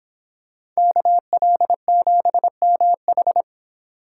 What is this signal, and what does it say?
Morse code, heard as a keyed tone, about 26 wpm, 700 Hz. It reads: KL7MH